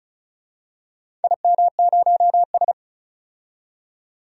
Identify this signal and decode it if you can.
Morse code, heard as a keyed tone, about 36 wpm, 700 Hz. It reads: IM0S
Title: DXCC-100 Call Signs - 4 Characters - 35wpm